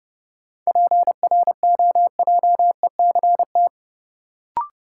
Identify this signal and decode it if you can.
Morse code, heard as a keyed tone, about 32 wpm, 700 Hz. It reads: PROJECT
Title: Sentences from Top 1400 Words - New Word Drill Alt - 30wpm